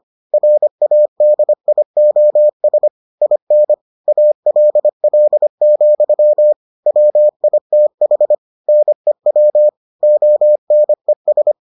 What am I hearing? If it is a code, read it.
Morse code, heard as a keyed tone, about 25 wpm, 600 Hz. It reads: RADIOS IN ALL, WITH NEW ONES